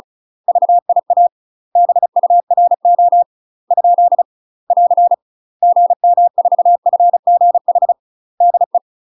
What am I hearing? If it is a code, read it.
Morse code, heard as a keyed tone, about 34 wpm, 700 Hz. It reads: VIA BURO ? <AR> GM4FGH DE
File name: UK MASTER QSOs TRAINING / 35WPM / UK MASTER QSOs 0003